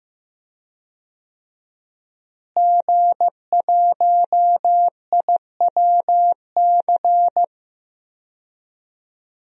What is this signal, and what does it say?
Morse code, heard as a keyed tone, about 15 wpm, 700 Hz. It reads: G1IWC
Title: DXCC-100 Call Signs - 5 Characters - Repeated 3x - 15wpm